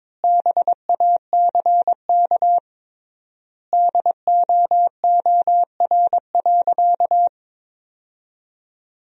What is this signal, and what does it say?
Morse code, heard as a keyed tone, about 22 wpm, 700 Hz. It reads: BACK DOOR.
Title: 3x Word-Spacing - Sentences from Top 300 Words - 22wpm